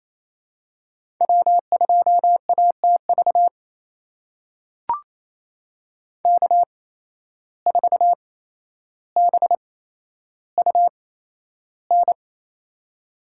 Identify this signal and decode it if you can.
Morse code, heard as a keyed tone, about 28 wpm, spaced as for 12 wpm, 700 Hz. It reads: W2ATV K4BUN
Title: USA Call Signs - 5 Characters - 8x Character Spacing - 28wpm